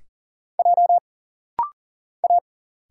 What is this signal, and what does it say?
Morse code, heard as a keyed tone, about 40 wpm, 700 Hz. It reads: J A
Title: Single Letter-Number Alt - Rapid-Fire 40wpm